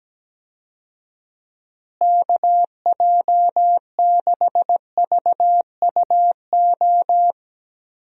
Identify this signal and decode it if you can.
Morse code, heard as a keyed tone, about 17 wpm, 700 Hz. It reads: KJ6VUO
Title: USA Call Signs - 6 Characters - Repeated 3x - 17wpm